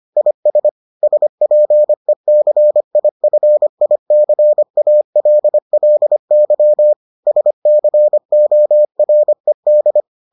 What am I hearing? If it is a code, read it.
Morse code, heard as a keyed tone, about 25 wpm, 600 Hz. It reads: IS SPECIFICALLY SCORED